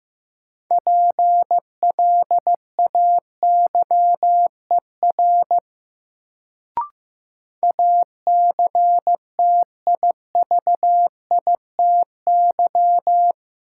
Morse code, heard as a keyed tone, about 15 wpm, 700 Hz. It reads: PLAYER ACTIVITY
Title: Sentences from Top 1000 Words - Review Part 4 - New Word Drill - 15wpm